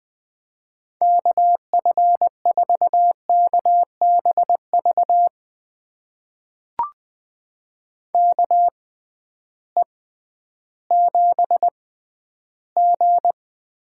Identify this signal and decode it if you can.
Morse code, heard as a keyed tone, about 20 wpm, spaced as for 12 wpm, 700 Hz. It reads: KF4KBV KE7G
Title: USA Call Signs - 6x Character Spacing - 20wpm